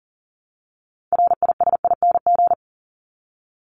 Morse code, heard as a keyed tone, about 40 wpm, 700 Hz. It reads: RISING
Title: ING Suffix - Top 100 Words with Base Words 40wpm